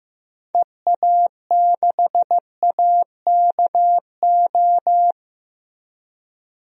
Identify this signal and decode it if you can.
Morse code, heard as a keyed tone, about 15 wpm, 700 Hz. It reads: EA6AKO
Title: DXCC-100 Call Signs - 6 Characters - Repeated 3x - 15wpm